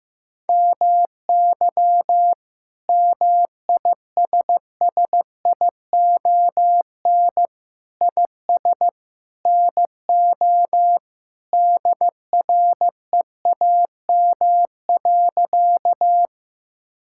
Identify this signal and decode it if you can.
Morse code, heard as a keyed tone, about 16 wpm, 700 Hz. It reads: MY MISSION IS NO DREAM.
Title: Sentences from Top 1600 Words 15wpm